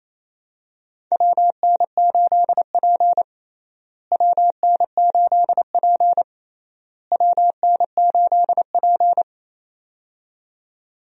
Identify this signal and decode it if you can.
Morse code, heard as a keyed tone, about 28 wpm, 700 Hz. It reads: WN8P WN8P WN8P
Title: USA Call Signs - 4 Characters - Repeated 3x - 28wpm